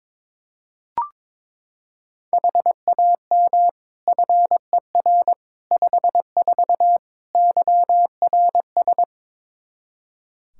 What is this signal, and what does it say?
Morse code, heard as a keyed tone, about 22 wpm, 700 Hz. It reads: HAM FER 54 YRS
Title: Practice QSO - Exchange 3 - More Info - 22wpm